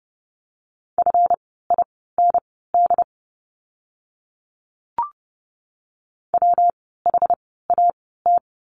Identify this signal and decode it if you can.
Morse code, heard as a keyed tone, about 30 wpm, spaced as for 17 wpm, 700 Hz. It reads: FIND WHAT
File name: course / 30:10wpm / 230 Lesson 37f - Letter x - Words 30wpm